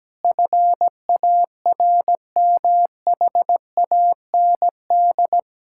Morse code, heard as a keyed tone, about 17 wpm, 700 Hz. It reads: FARMHAND
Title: Pangram Challenge v2 - 17wpm